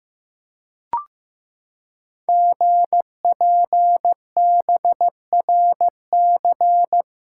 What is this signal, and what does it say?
Morse code, heard as a keyed tone, about 15 wpm, 700 Hz. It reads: GPBRC
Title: Code Groups - 5 Letters - 15wpm